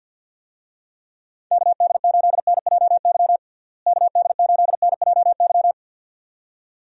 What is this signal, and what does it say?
Morse code, heard as a keyed tone, about 46 wpm, 700 Hz. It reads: KD8NJY KD8NJY
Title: USA Call Signs - 6 Characters - Repeated 2x - 50wpm